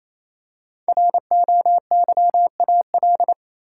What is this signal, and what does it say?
Morse code, heard as a keyed tone, about 28 wpm, 700 Hz. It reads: ROYAL
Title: Five Letter Words - 28wpm